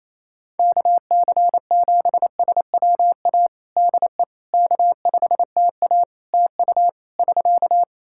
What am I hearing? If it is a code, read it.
Morse code, heard as a keyed tone, about 27 wpm, 700 Hz. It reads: KC7SWA DE K5TA TU <SK>
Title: Practice QSO - Exchange 4 - Ending - 28wpm